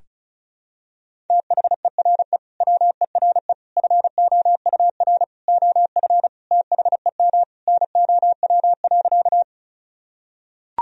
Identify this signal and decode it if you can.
Morse code, heard as a keyed tone, about 36 wpm, 700 Hz. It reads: THERE WERE FOUR OF THEM NOW.
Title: Sentences from Top 1000 Words - Review Part 1 - 35wpm